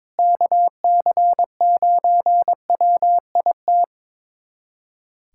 Morse code, heard as a keyed tone, about 22 wpm, 700 Hz. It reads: KC9WIT
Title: USA Callsign DE Callsign 22wpm